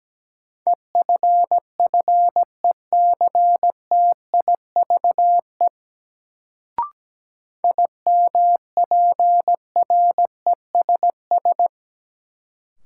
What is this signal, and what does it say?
Morse code, heard as a keyed tone, about 17 wpm, 700 Hz. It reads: EFFECTIVE IMPRESS
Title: Sentences from Top 1600 Words - New Word Drill Alt - 17wpm